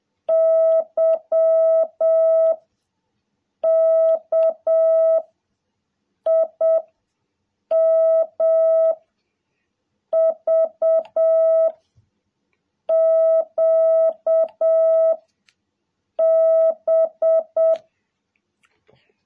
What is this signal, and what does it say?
Morse code, heard as a keyed tone, about 7 wpm, 650 Hz. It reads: Y K I M V Q B
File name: jh8cbh / 2021cw